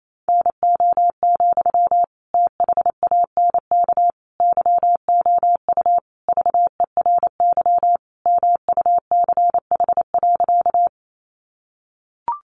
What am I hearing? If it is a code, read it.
Morse code, heard as a keyed tone, about 28 wpm, 700 Hz. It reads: NO, THANK YOU VERY MUCH.
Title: Sentences from Top 600 Words 28wpm